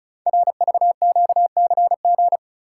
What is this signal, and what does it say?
Morse code, heard as a keyed tone, about 36 wpm, 700 Hz. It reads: RVQCG
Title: Code Groups - 5 Letters - 35wpm